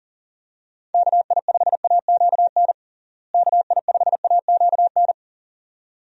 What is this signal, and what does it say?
Morse code, heard as a keyed tone, about 37 wpm, 700 Hz. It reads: KI5AQN KI5AQN
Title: USA Call Signs - 6 Characters - Repeated 2x - 40wpm